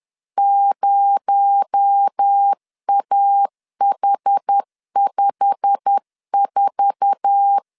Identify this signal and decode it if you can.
Morse code, heard as a keyed tone, about 10 wpm, 800 Hz. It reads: 0AH54
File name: LEZIONI / 8 YJ45 / 8 YJ45 10wpm 2549